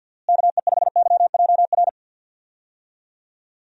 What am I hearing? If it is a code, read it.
Morse code, heard as a keyed tone, about 46 wpm, 700 Hz. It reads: KE5YJR
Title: USA Call Signs - 6 Characters - Repeated 3x - 50wpm